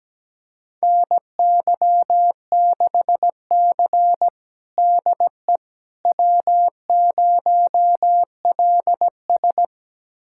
Morse code, heard as a keyed tone, about 17 wpm, 700 Hz. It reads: NY6C DE W0LS
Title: USA Callsign DE Callsign - 4 Characters - 17wpm